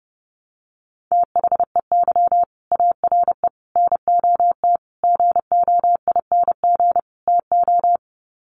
Morse code, heard as a keyed tone, about 30 wpm, 700 Hz. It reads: THEY ARE NOT GOING TO
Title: Sets of 5 Words Encore2 30wpm